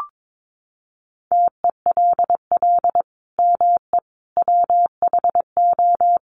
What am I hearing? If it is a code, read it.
Morse code, heard as a keyed tone, about 22 wpm, 700 Hz. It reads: TELL ME WHO
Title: Sentences from Top 200 Words 22wpm